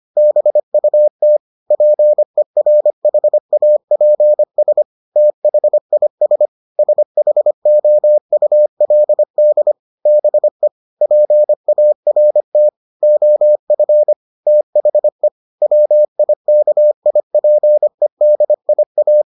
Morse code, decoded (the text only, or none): BUT PERHAPS THIS SHOULD BE PART OF THE WIKIPEDIA